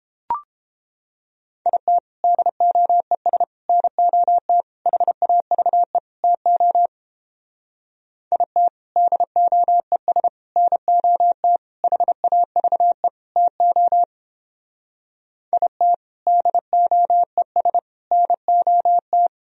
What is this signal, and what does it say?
Morse code, heard as a keyed tone, about 29 wpm, 700 Hz. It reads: IT DOES NOT HAVE TO IT DOES NOT HAVE TO IT DOES NOT